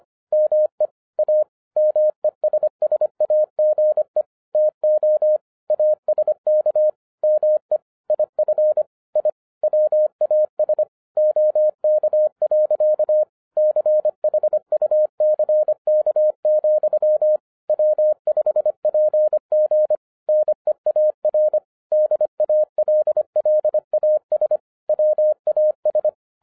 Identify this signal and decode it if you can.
Morse code, heard as a keyed tone, about 25 wpm, 610 Hz. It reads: ME A MESSAGE TO ASK ME IF I WAS OK. CHUCK, W5PG NEAR DALLAS WAS
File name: 2011-07-mp3 / n34 THE 3.11 2 25wpm